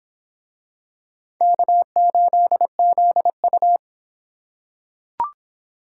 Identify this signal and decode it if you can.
Morse code, heard as a keyed tone, about 25 wpm, 700 Hz. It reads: K8ZU